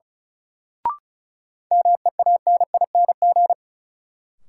Morse code, heard as a keyed tone, about 36 wpm, 700 Hz. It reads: MEANING
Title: Sentences from Top 900 Words - New Word Drill - 35wpm